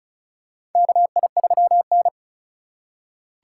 Morse code, heard as a keyed tone, about 36 wpm, 700 Hz. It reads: KI3N
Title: USA Call Signs - 4 Characters - 3x Character Spacing - 35wpm